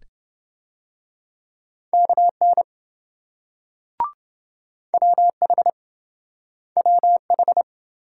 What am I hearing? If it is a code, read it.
Morse code, heard as a keyed tone, about 26 wpm, 700 Hz. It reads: KN WH WH